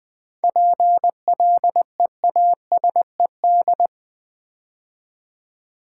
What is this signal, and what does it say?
Morse code, heard as a keyed tone, about 20 wpm, 700 Hz. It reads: PLEASED
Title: 7 Letter Words - First 3 Letters Given - 20wpm